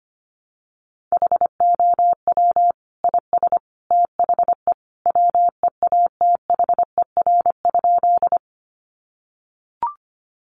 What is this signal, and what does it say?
Morse code, heard as a keyed tone, about 25 wpm, 700 Hz. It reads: HOW IS THE WEATHER?
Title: Sentences from Top 800 Words 25wpm